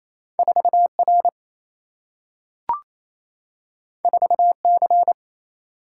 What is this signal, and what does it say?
Morse code, heard as a keyed tone, about 28 wpm, 700 Hz. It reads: 4R 4C
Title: Single-Letter Single-Number 28wpm